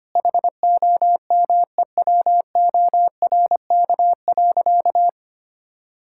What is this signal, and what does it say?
Morse code, heard as a keyed tone, about 25 wpm, 700 Hz. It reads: HOMEWORK.
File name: Easy Sentences with Hints - 25wpm